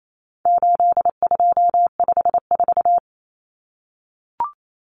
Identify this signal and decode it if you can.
Morse code, heard as a keyed tone, about 28 wpm, 700 Hz. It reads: 8254
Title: Common Cut Numbers - 4-Digit 28wpm